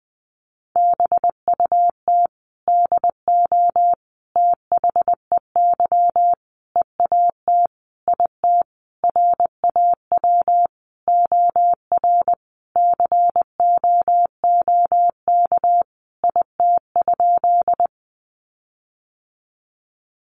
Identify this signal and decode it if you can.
Morse code, heard as a keyed tone, about 20 wpm, 700 Hz. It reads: BUT DO THEY EAT IT RAW OR COOK IT?
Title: Sentences from Top 1500 Words 20wpm